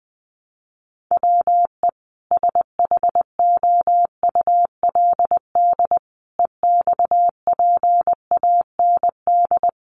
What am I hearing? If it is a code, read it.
Morse code, heard as a keyed tone, about 20 wpm, 700 Hz. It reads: WE SHOULD EXPAND
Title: Sentences from Top 1400 Words 20wpm